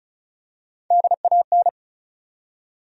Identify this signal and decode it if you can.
Morse code, heard as a keyed tone, about 35 wpm, 700 Hz. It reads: DAN
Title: USA Common Names - Max 5 Letters - 35wpm